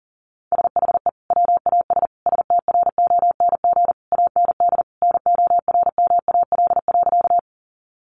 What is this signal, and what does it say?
Morse code, heard as a keyed tone, about 40 wpm, 700 Hz. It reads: SHE WAS STRONG AND NORMAL.
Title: Sentences from Top 1200 Words 40wpm